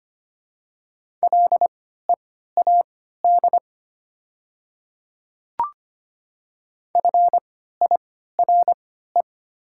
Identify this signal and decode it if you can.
Morse code, heard as a keyed tone, about 25 wpm, spaced as for 14 wpm, 700 Hz. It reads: LEAD FIRE
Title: 3x Character Spacing - Top 500 Words - 25wpm